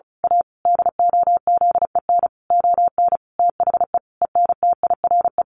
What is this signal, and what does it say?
Morse code, heard as a keyed tone, about 34 wpm, 700 Hz. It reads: A DOZEN ON THE ENTIRE
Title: A Sound Of Thunder 35wpm